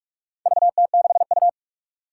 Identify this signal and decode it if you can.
Morse code, heard as a keyed tone, about 46 wpm, 700 Hz. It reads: VT6U
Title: Four Challenging Letter-Number Combinations 45wpm